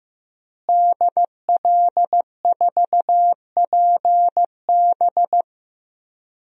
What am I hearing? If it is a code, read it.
Morse code, heard as a keyed tone, about 15 wpm, 700 Hz. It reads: DL4PB
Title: DXCC-100 Call Signs - 5 Characters - Repeated 2x - 15wpm